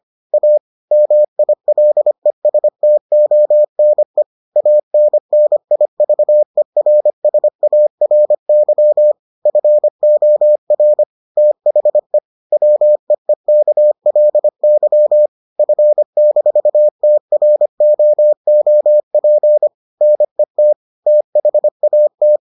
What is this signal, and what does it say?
Morse code, heard as a keyed tone, about 25 wpm, 600 Hz. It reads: A MILESTONE ANNIVERSARY FOR THE WEEKLY F-TROOP NET THAT